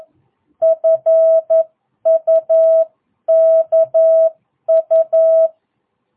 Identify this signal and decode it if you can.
Morse code, heard as a keyed tone, about 11 wpm, 660 Hz. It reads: FUKU
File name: jh8cbh / 2022cw